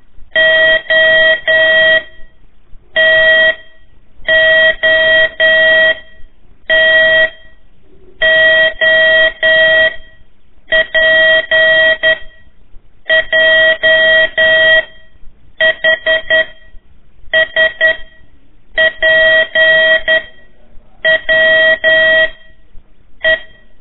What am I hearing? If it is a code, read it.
Morse code, heard as a keyed tone, about 11 wpm, 1900 Hz, with noise. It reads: O T O T O P J H S P W E
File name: 3 lezione CW 1 AIR